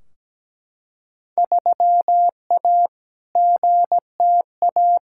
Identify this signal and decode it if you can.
Morse code, heard as a keyed tone, about 17 wpm, 700 Hz. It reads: 3A GTA
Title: Field Day Exchanges - 17wpm